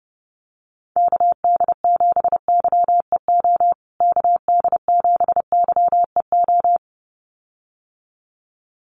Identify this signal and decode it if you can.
Morse code, heard as a keyed tone, about 30 wpm, 700 Hz. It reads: KD7YEO KD7YEO
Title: USA Call Signs Repeated 2x 30wpm